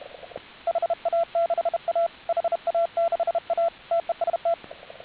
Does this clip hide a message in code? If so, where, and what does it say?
Yes: Morse code from 0.7 to 4.5 s, heard as a keyed tone, about 31 wpm, 690 Hz, with radio noise. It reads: HA6A HA6A TEST